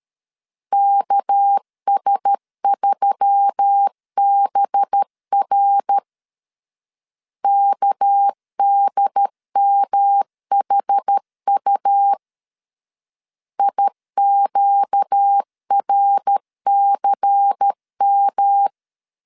Morse code, heard as a keyed tone, about 13 wpm, 800 Hz. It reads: KS3BR KDMHU IQRCM
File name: LEZIONI / 6 PKX0 / 6 PK 12wpm 2006